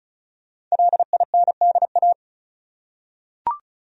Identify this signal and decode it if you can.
Morse code, heard as a keyed tone, about 34 wpm, 700 Hz. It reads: LINDA
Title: USA Common Names - Max 5 Letters - 35wpm